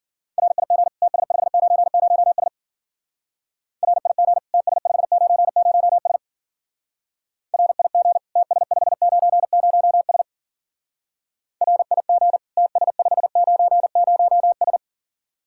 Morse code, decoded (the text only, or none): RIG TS590S RIG TS590S RIG TS590S RIG TS590S